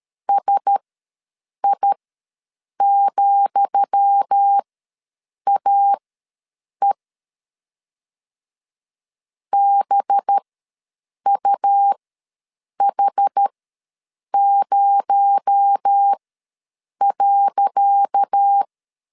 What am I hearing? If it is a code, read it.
Morse code, heard as a keyed tone, about 13 wpm, spaced as for 7 wpm, 800 Hz. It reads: SI,AE BUH0.
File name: LEZIONI / 10 SIGNS / 10D COMA ws 12wpm 3618